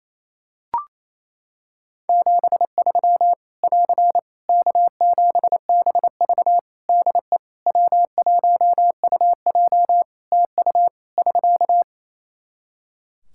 Morse code, heard as a keyed tone, about 28 wpm, 700 Hz. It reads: 73 <AR> K7BV DE W1UJ TU <SK>
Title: K1USN SST QSO Practice 28wpm